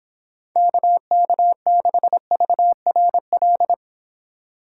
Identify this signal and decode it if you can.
Morse code, heard as a keyed tone, about 25 wpm, 700 Hz. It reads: KK6VRL